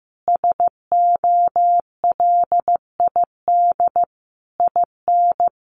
Morse code, heard as a keyed tone, about 14 wpm, 700 Hz. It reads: SOLID IN
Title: Sentences from Top 1500 Words 15wpm